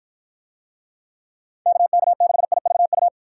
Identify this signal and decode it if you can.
Morse code, heard as a keyed tone, about 47 wpm, 700 Hz. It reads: KK6IVU